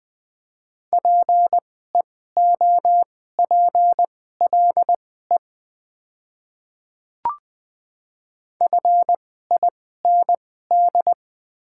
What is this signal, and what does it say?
Morse code, heard as a keyed tone, about 20 wpm, spaced as for 15 wpm, 700 Hz. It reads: PEOPLE FIND
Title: 2x Character Spacing - Top 100 Words - 20wpm